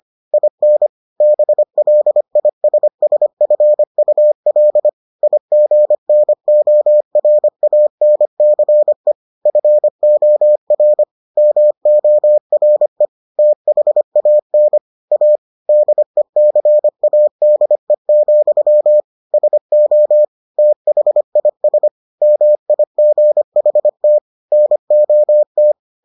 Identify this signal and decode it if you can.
Morse code, heard as a keyed tone, about 25 wpm, 600 Hz. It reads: IN BLISSFUL IGNORANCE FOR MORE THAN A DECADE, SO THIS MIGHT NOT